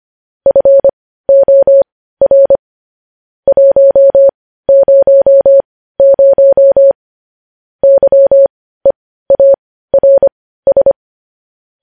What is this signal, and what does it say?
Morse code, heard as a keyed tone, about 25 wpm, spaced as for 15 wpm, 550 Hz. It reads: FOR 100 YEARS
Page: News Headlines in Morse Code at 15 WPM